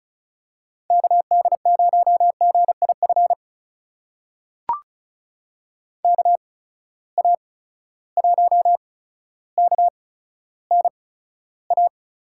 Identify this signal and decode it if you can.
Morse code, heard as a keyed tone, about 36 wpm, spaced as for 15 wpm, 700 Hz. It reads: KD0GIF KA1KNA
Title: USA Call Signs - 6 Characters - 8x Character Spacing - 35wpm